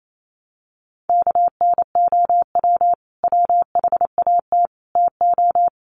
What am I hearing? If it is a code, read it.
Morse code, heard as a keyed tone, about 28 wpm, 700 Hz. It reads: KNOW WHAT TO
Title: Sets of 5 Words 28wpm